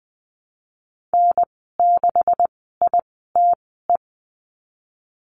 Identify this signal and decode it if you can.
Morse code, heard as a keyed tone, about 19 wpm, spaced as for 14 wpm, 700 Hz. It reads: N6ITE
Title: 128 Lesson 21g - Number 6 - Call Signs 20wpm